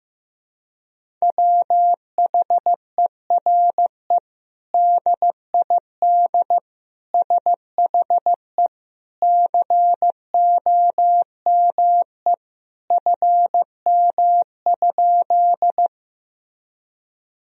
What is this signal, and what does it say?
Morse code, heard as a keyed tone, about 16 wpm, 700 Hz. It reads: WHERE DID SHE COME FM?
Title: Sentences from Top 200 Words and Abbreviations 15wpm